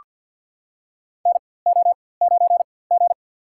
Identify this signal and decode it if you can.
Morse code, heard as a keyed tone, about 47 wpm, spaced as for 23 wpm, 700 Hz. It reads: NO9G
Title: USA Call Signs - 4x Character Spacing - 50wpm